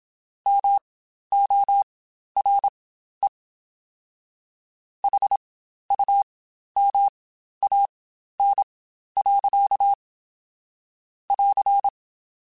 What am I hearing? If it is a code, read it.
Morse code, heard as a keyed tone, about 27 wpm, spaced as for 12 wpm, 800 Hz. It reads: MORE HUMAN. <AR>
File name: Spaced / Chuckles Spaced-mp3s / HumanErr sp @25